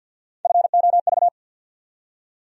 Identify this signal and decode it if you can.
Morse code, heard as a keyed tone, about 49 wpm, 700 Hz. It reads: FOV
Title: USA Call Sign Postfixes - 3 Letters 50wpm